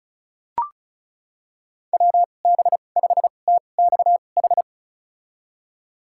Morse code, heard as a keyed tone, about 34 wpm, spaced as for 25 wpm, 700 Hz. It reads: WB5TXH
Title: USA Call Signs - 6 Characters - 2x Character Spacing - 35wpm